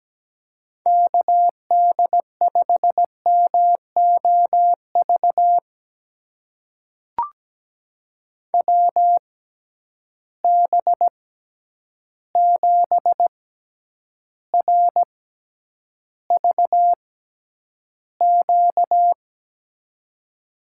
Morse code, heard as a keyed tone, about 17 wpm, spaced as for 9 wpm, 700 Hz. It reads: KD5MOV WB7RVQ
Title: USA Call Signs - 6 Characters - 6x Character Spacing - 17wpm